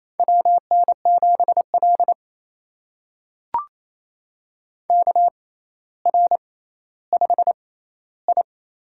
Morse code, heard as a keyed tone, about 28 wpm, spaced as for 14 wpm, 700 Hz. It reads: WN7L KR5I
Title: USA Call Signs - 4 Characters - 6x Character Spacing - 28wpm